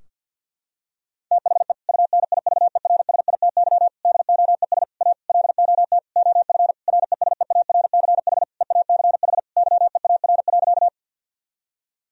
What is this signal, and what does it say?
Morse code, heard as a keyed tone, about 51 wpm, 700 Hz. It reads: THE UNIVERSITY DOES A LOT OF RESEARCH EACH YEAR.
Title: Sentences from Top 1000 Words - Review Part 4 - 50wpm